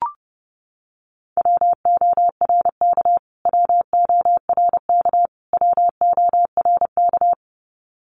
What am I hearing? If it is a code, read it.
Morse code, heard as a keyed tone, about 30 wpm, 700 Hz. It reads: WORK WORK WORK
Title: Top 200 Words Repeated 30wpm